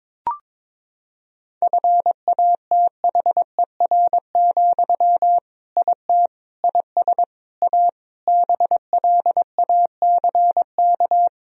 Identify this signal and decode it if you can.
Morse code, heard as a keyed tone, about 22 wpm, 700 Hz. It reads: FATHER, IT IS A BLACK
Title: Sentences from Top 1000 Words - Review Part 2 - 22wpm